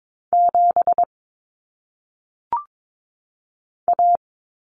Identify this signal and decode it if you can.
Morse code, heard as a keyed tone, about 22 wpm, 700 Hz. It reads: 7 A
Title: Common Cut Numbers - 1-Digit 22wpm